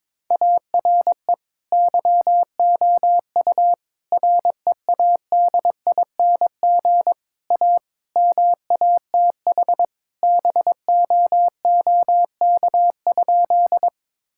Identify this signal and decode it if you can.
Morse code, heard as a keyed tone, about 22 wpm, 700 Hz. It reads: ARE YOU READING A MATH BOOK?
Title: Sentences from Top 1000 Words - Review Part 6 - 22wpm